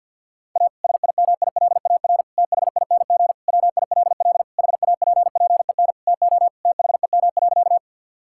Morse code, heard as a keyed tone, about 46 wpm, 700 Hz. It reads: A SIMILAR THING WILL HAPPEN TO THEM.
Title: Sentences from Top 1000 Words - Review Part 5 - 50wpm